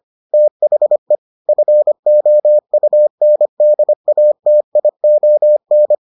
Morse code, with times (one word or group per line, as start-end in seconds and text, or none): 0.3-1.1 THE
1.5-6.0 FOUNDATION